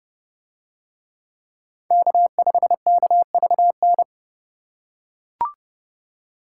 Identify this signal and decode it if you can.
Morse code, heard as a keyed tone, about 28 wpm, 700 Hz. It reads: K5KVN